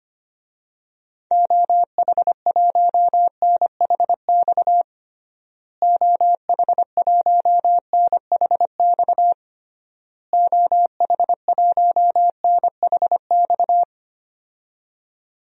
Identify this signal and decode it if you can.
Morse code, heard as a keyed tone, about 25 wpm, 700 Hz. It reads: OH1NHX OH1NHX OH1NHX
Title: DXCC-100 Call Signs - 6 Characters - Repeated 3x - 25wpm